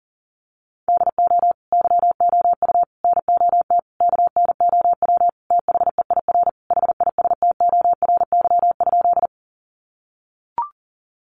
Morse code, heard as a keyed tone, about 40 wpm, 700 Hz. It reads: DO YOU NOT KNOW THEIR HISTORY?
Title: Sentences from Top 600 Words 40wpm